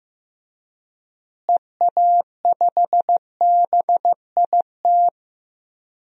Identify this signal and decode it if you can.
Morse code, heard as a keyed tone, about 16 wpm, 700 Hz. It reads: EA5BIT
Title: DXCC-100 Call Signs - 6 Characters - Repeated 2x - 15wpm